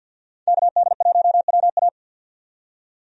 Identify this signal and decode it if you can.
Morse code, heard as a keyed tone, about 52 wpm, 700 Hz. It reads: KD1WA
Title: USA Call Signs - 5 Characters - 50wpm